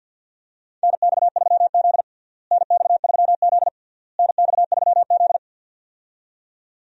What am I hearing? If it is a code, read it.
Morse code, heard as a keyed tone, about 50 wpm, 700 Hz. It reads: NX3Z NX3Z NX3Z
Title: USA Call Signs - 4 Characters - Repeated 3x - 50wpm